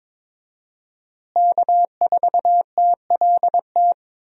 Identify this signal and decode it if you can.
Morse code, heard as a keyed tone, about 22 wpm, 700 Hz. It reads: K4TLT
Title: USA Call Signs - 5 Characters - Repeated 3x - 22wpm